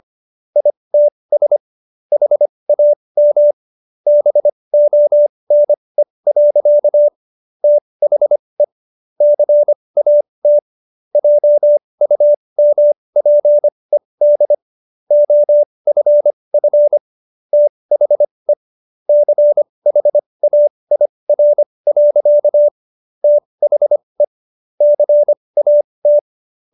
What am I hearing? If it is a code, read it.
Morse code, heard as a keyed tone, about 25 wpm, spaced as for 20 wpm, 590 Hz. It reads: ITS HAM BONE. THE CAT JUMPED OFF THE CHAIR. THE CAT